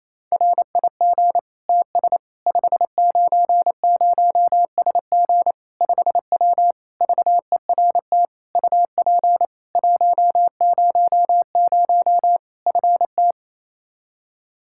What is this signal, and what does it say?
Morse code, heard as a keyed tone, about 28 wpm, 700 Hz. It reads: RIG TS 590SG 5W VERT UP 100 FT
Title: Practice QSO - Exchange 2 - Info - 28wpm